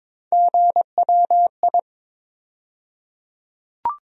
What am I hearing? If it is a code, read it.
Morse code, heard as a keyed tone, about 22 wpm, 700 Hz. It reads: GWI
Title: USA Call Sign Postfixes - 3 Letters 22wpm